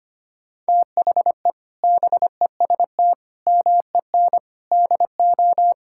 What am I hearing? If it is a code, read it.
Morse code, heard as a keyed tone, about 25 wpm, 700 Hz. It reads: THE BEST MEN DO
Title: Sentences from Top 1000 Words - Review Part 6 - 25wpm